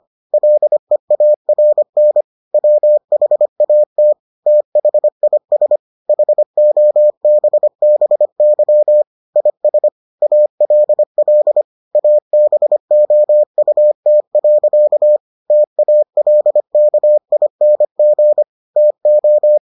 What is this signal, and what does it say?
Morse code, heard as a keyed tone, about 25 wpm, 600 Hz. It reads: LEARN WHAT THIS HOBBY IS ALL ABOUT. TALKING TO